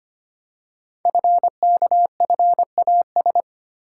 Morse code, heard as a keyed tone, about 25 wpm, 700 Hz. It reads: FKFAS
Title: Code Groups - 5 Letter:Numbers - 25wpm